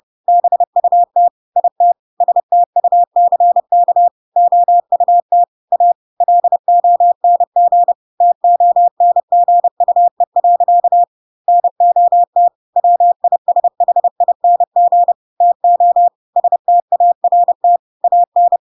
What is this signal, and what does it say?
Morse code, heard as a keyed tone, about 30 wpm, 700 Hz. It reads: BUT IT STUCK OUT A LONG TONGUE. NOT WISHING TO START AN
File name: Pinocchio / Pinocchio ch3 30 30 ik1hge